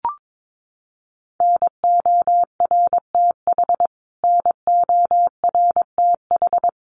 Morse code, heard as a keyed tone, about 22 wpm, 700 Hz. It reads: NORTH NORTH
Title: Top 500 Words Repeated 22wpm